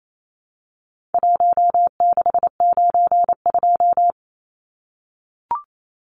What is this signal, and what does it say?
Morse code, heard as a keyed tone, about 28 wpm, 700 Hz. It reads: 1692
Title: Numbers - 4-Digit 28wpm